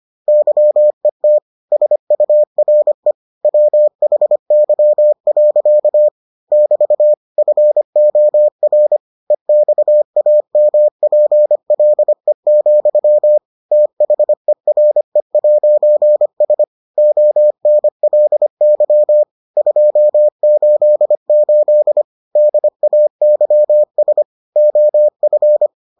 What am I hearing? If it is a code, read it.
Morse code, heard as a keyed tone, about 25 wpm, 600 Hz. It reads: YET SURE WHY. <BT> FOR EXAMPLE, THERE'S ONLY 288 DAYS OF